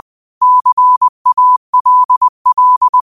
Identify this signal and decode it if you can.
Morse code, heard as a keyed tone, about 20 wpm, 1000 Hz. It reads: CALL
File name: qotdfiles / 20wpm